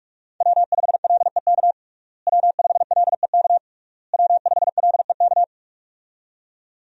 Morse code, heard as a keyed tone, about 41 wpm, 700 Hz. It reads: W5LEK W5LEK W5LEK
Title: USA Call Signs - 5 Characters - Repeated 3x - 45wpm